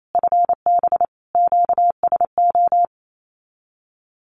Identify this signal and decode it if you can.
Morse code, heard as a keyed tone, about 28 wpm, 700 Hz. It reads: FB QSO
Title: Common QSO Phrases Encore 28wpm